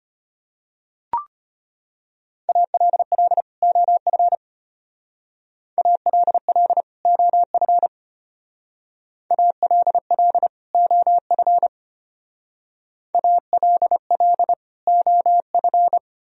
Morse code, heard as a keyed tone, about 31 wpm, 700 Hz. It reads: ALL OF ALL OF ALL OF ALL OF
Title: Speed-Racing - Sets of 2 Words 25wpm